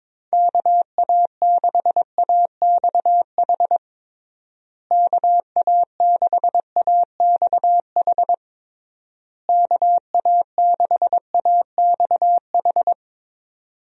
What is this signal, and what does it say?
Morse code, heard as a keyed tone, about 22 wpm, 700 Hz. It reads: KA6AXH KA6AXH KA6AXH
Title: USA Call Signs - 6 Characters - Repeated 3x - 22wpm